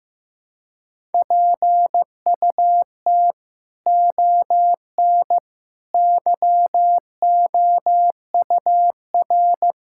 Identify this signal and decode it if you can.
Morse code, heard as a keyed tone, about 15 wpm, 700 Hz. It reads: PUT ON YOUR
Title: Sentences from Top 1600 Words 15wpm